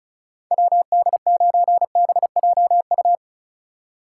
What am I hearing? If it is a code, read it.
Morse code, heard as a keyed tone, about 36 wpm, 700 Hz. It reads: WD9BJU
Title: USA Call Signs - 6 Characters - 3x Character Spacing - 35wpm